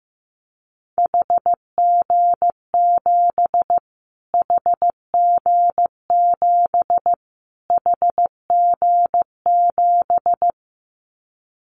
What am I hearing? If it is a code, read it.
Morse code, heard as a keyed tone, about 15 wpm, 700 Hz. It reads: HG7 HG7 HG7
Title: Callsign Prefixes for Top 25 DXCC - Repeated 3x - 15wpm